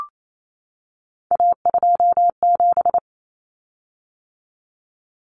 Morse code, heard as a keyed tone, about 28 wpm, 700 Hz. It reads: A27
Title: Common Cut Numbers - 3-Digit 28wpm